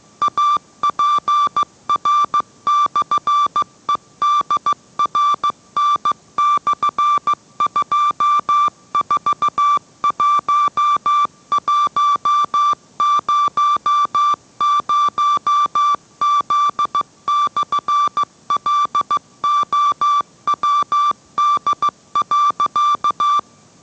Morse code, heard as a keyed tone, about 15 wpm, 1170 Hz. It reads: APR/EDRN/241100Z/LOWD.